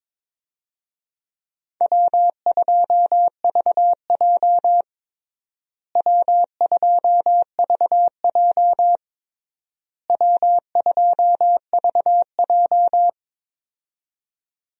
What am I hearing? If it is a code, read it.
Morse code, heard as a keyed tone, about 21 wpm, 700 Hz. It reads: W2VJ W2VJ W2VJ
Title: USA Call Signs - 4 Characters - Repeated 3x - 22wpm